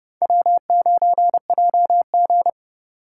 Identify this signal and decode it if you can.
Morse code, heard as a keyed tone, about 29 wpm, 700 Hz. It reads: W9JG